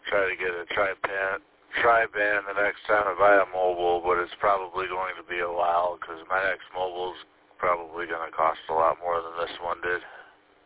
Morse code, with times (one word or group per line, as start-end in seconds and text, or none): none